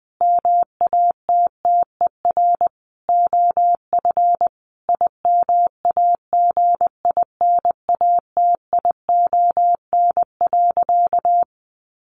Morse code, heard as a keyed tone, about 20 wpm, 700 Hz. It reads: MATTER OF IMAGINATION.
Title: Sentences from Top 1300 Words 20wpm